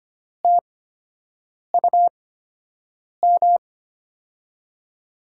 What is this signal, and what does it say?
Morse code, heard as a keyed tone, about 25 wpm, spaced as for 7 wpm, 700 Hz. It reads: TUM
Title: USA Call Signs - 5 Characters - 8x Character Spacing - 25wpm